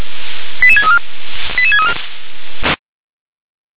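Five-tone selective call (ZVEI), heard as quick runs of five tones, tones of about 70 ms. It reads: 88634 88634
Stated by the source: ZVEI five-tone selective call received on SDR.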